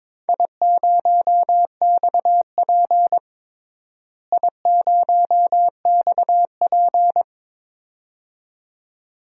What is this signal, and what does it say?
Morse code, heard as a keyed tone, about 22 wpm, 700 Hz. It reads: I0XP I0XP
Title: DXCC-100 Call Signs - 4 Characters - Repeated 2x - 22wpm